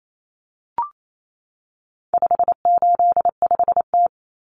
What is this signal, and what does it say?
Morse code, heard as a keyed tone, about 28 wpm, 700 Hz. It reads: 585T
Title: Common Cut Numbers - 4-Digit 28wpm